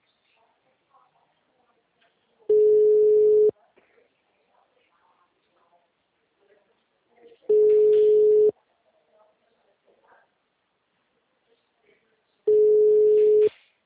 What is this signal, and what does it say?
Ringback tone (Europe, 425 Hz), with speech.
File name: out- -2009- - - .111